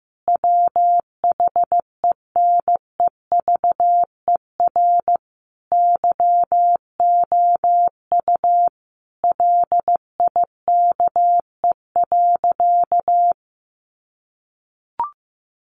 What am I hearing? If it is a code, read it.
Morse code, heard as a keyed tone, about 15 wpm, 700 Hz. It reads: WHENEVER YOU LIKE.
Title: Sentences from Top 1200 Words 15wpm